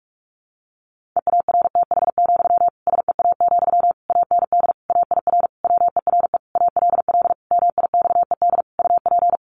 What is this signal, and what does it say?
Morse code, heard as a keyed tone, about 44 wpm, 700 Hz. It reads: EARTH, SEA, AND AIR WERE ALL MIXED UP
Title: Sentences from Top 1500 Words 45wpm